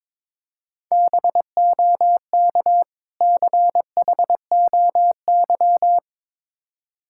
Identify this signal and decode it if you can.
Morse code, heard as a keyed tone, about 23 wpm, 690 Hz. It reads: BOK CHOY
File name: Difficult Words with Hints - 22wpm